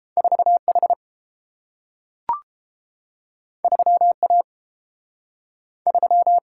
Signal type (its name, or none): Morse code